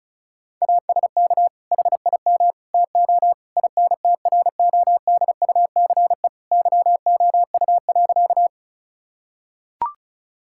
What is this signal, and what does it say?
Morse code, heard as a keyed tone, about 34 wpm, 700 Hz. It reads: ASK HIM TO INTRODUCE YOU.
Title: Sentences from Top 1600 Words 35wpm